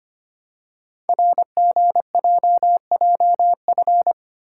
Morse code, heard as a keyed tone, about 25 wpm, 700 Hz. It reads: RGJJF
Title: Code Groups - 5 Letters - 25wpm